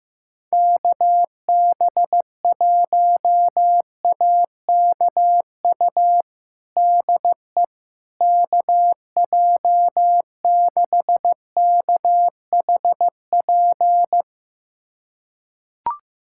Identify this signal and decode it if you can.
Morse code, heard as a keyed tone, about 16 wpm, 700 Hz. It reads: KB1AKU DE KJ6KHP
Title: USA Callsign DE Callsign - 6 Characters - 15wpm